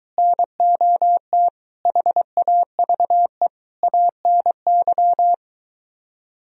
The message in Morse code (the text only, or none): NOT HAVE ANY